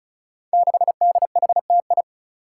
Morse code, heard as a keyed tone, about 36 wpm, 700 Hz. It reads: 6DHTI
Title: Five Challenging Letter-Number Combinations 35wpm